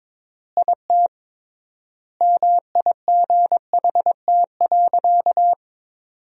Morse code, heard as a keyed tone, about 21 wpm, 700 Hz. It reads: IT MIGHT.
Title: 3x Word-Spacing - Sentences from Top 500 Words - 22wpm